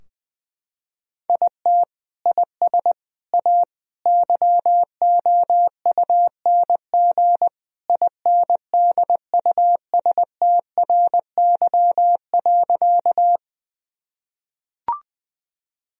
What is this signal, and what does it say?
Morse code, heard as a keyed tone, about 20 wpm, 700 Hz. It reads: IT IS A YOUNG INDUSTRY.
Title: Sentences from Top 1000 Words - Review Part 4 - 20wpm